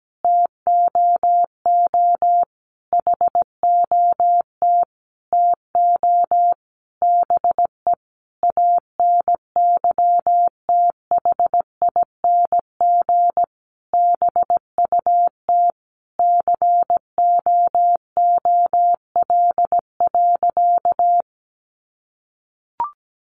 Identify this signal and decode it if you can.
Morse code, heard as a keyed tone, about 17 wpm, 700 Hz. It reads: TOO HOT TO BE ANYTHING BUT COOL.
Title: Sentences from Top 1200 Words 17wpm